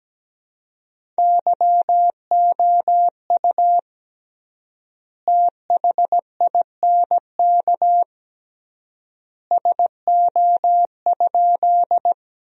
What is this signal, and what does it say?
Morse code, heard as a keyed tone, about 17 wpm, 700 Hz. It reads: YOU THINK SO?
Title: 3x Word-Spacing - Sentences from Top 200 Words - 17wpm